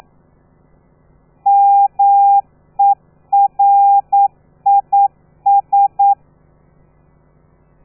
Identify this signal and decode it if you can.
Morse code, heard as a keyed tone, about 9 wpm, 800 Hz. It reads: MERIS
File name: telegrafia / i2tfj / Lez13 40